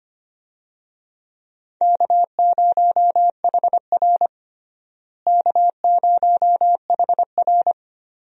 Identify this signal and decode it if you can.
Morse code, heard as a keyed tone, about 25 wpm, 700 Hz. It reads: K0HR K0HR
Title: USA Call Signs - 4 Characters - Repeated 3x - 25wpm